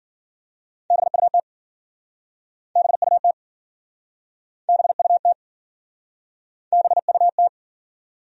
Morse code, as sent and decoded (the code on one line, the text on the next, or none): -... ..- - / -... ..- - / -... ..- - / -... ..- -
BUT BUT BUT BUT